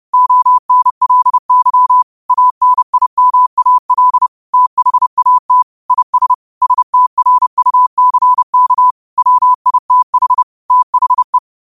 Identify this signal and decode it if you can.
Morse code, heard as a keyed tone, about 30 wpm, 1000 Hz. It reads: ONLY ANIMAL THAT IS STRUCK WITH THE